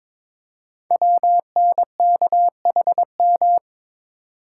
Morse code, heard as a keyed tone, about 22 wpm, 700 Hz. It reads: WNKHM
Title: Code Groups - 5 Letters - 22wpm